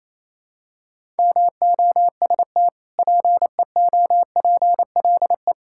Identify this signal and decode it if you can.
Morse code, heard as a keyed tone, about 28 wpm, 700 Hz. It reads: MOST PEOPLE
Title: Sentences from Top 1000 Words - Review Part 5 - 28wpm